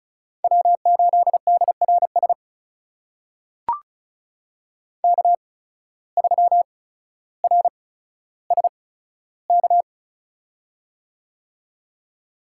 Morse code, heard as a keyed tone, about 36 wpm, spaced as for 15 wpm, 700 Hz. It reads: W8DRS K3RSK
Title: USA Call Signs - 5 Characters - 8x Character Spacing - 35wpm